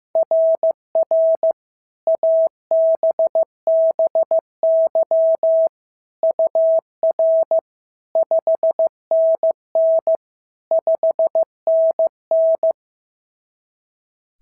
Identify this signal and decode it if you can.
Morse code, heard as a keyed tone, about 16 wpm, 690 Hz. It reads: RR ABBY UR 5NN 5NN
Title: SOTA QSO Practice 15wpm